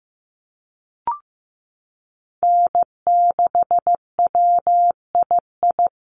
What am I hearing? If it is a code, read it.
Morse code, heard as a keyed tone, about 15 wpm, 700 Hz. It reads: N6WII
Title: USA Call Signs Repeated 3x 15wpm